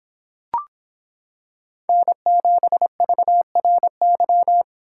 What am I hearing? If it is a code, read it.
Morse code, heard as a keyed tone, about 25 wpm, 700 Hz. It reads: N7VRY